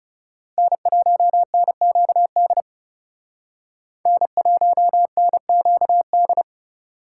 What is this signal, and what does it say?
Morse code, heard as a keyed tone, about 38 wpm, 700 Hz. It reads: N1NQD N1NQD